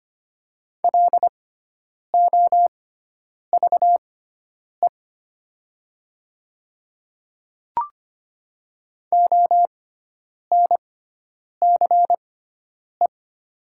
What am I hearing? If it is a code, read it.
Morse code, heard as a keyed tone, about 25 wpm, spaced as for 9 wpm, 700 Hz. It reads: LOVE ONCE
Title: 6x Character Spacing - Top 500 Words - 25wpm